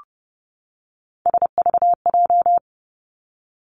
Morse code, heard as a keyed tone, about 31 wpm, 700 Hz. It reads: SVJ
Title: Three Challenging Letter-Number Combinations 30wpm